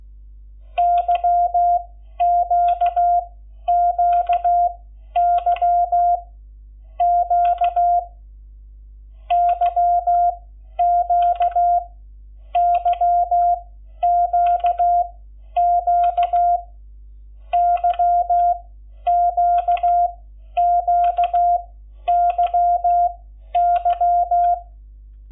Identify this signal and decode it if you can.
Morse code, heard as a keyed tone, about 16 wpm, spaced as for 11 wpm, 700 Hz. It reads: YQQY Q YQ YQQ YQQYY
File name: Lecciones / CLASE Nº11